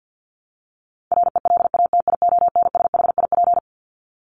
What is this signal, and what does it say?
Morse code, heard as a keyed tone, about 48 wpm, 700 Hz. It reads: RELATIONSHIP
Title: Top 1001-1250 Words 50wpm